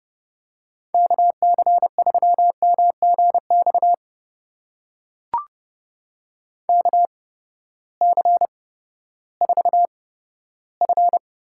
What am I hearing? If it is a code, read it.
Morse code, heard as a keyed tone, about 31 wpm, spaced as for 15 wpm, 700 Hz. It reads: KC3MGX KC4F
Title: USA Call Signs - 6 Characters - 8x Character Spacing - 30wpm